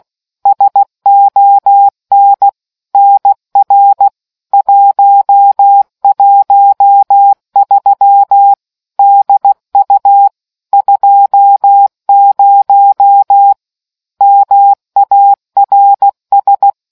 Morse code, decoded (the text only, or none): SON NR 113 DU 20 MARS